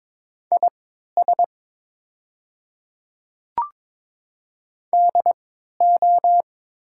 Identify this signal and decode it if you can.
Morse code, heard as a keyed tone, about 22 wpm, spaced as for 12 wpm, 700 Hz. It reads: IS DO
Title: 3x Character Spacing - Top 200 Words - 22wpm